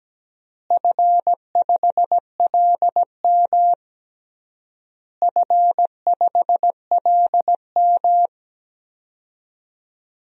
Morse code, heard as a keyed tone, about 17 wpm, 700 Hz. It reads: F5LM F5LM
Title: DXCC-100 Call Signs - 4 Characters - Repeated 3x - 17wpm